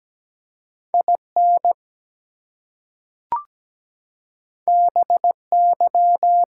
Morse code, heard as a keyed tone, about 17 wpm, 700 Hz. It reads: IN BY
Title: Two Letter Words - 17wpm